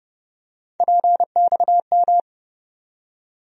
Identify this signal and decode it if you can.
Morse code, heard as a keyed tone, about 31 wpm, 700 Hz. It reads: PXM
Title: USA Call Sign Postfixes - 3 Letters 30wpm